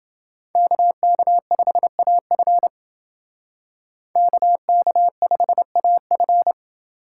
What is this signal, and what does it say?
Morse code, heard as a keyed tone, about 31 wpm, 700 Hz. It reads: KK5AF KK5AF